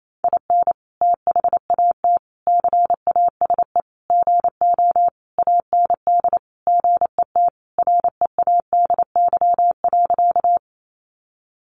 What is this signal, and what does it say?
Morse code, heard as a keyed tone, about 28 wpm, 700 Hz. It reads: IN THAT CASE GO AND GET READY.
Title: Sentences from Top 700 Words 28wpm